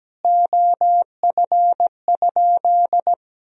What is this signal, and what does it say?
Morse code, heard as a keyed tone, about 17 wpm, 700 Hz. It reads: OF?
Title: Sentences from Top 200 Words and Abbreviations 17wpm